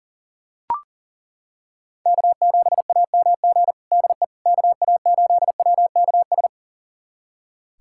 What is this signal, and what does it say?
Morse code, heard as a keyed tone, about 40 wpm, 700 Hz. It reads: K7AMG DE KA8WKS
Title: USA Callsign DE Callsign 40wpm